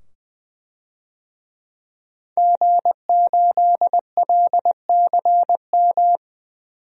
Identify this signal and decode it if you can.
Morse code, heard as a keyed tone, about 20 wpm, 700 Hz. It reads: G8LCM
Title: DXCC-100 Call Signs - 5 Characters - 20wpm